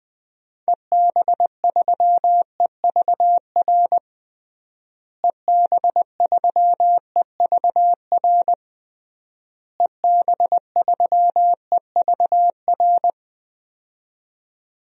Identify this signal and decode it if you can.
Morse code, heard as a keyed tone, about 21 wpm, 700 Hz. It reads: EB3EVR EB3EVR EB3EVR
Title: DXCC-100 Call Signs - 6 Characters - Repeated 3x - 20wpm